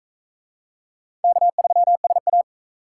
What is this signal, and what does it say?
Morse code, heard as a keyed tone, about 36 wpm, 700 Hz. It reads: K3SA